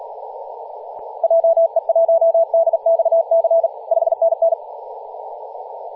Morse code, heard as a keyed tone, about 38 wpm, 660 Hz, with radio noise. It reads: JE1DXC 5NN
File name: posts / 2016-aug-26-2236z-yj0cow-18-073mhz-cw